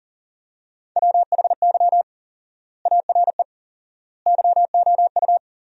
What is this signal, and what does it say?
Morse code, heard as a keyed tone, about 40 wpm, 700 Hz. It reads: WHY ARE YOU
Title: 4x Word-Spacing - Sentences from Top 200 Words - 40wpm